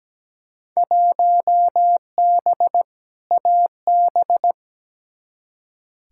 Morse code, heard as a keyed tone, about 17 wpm, 700 Hz. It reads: 1B AB
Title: Field Day Exchanges - 17wpm